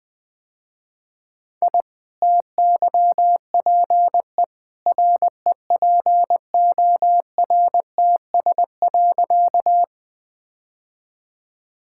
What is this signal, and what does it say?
Morse code, heard as a keyed tone, about 20 wpm, 700 Hz. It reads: I TYPE REPORTS.
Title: Easy Sentences with Hints - 20wpm